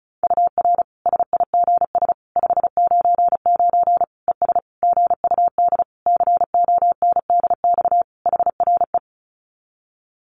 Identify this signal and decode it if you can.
Morse code, heard as a keyed tone, about 34 wpm, 700 Hz. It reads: UR SIGS 599 ES GUD CONDX HRE
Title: Common QSO Phrases 35wpm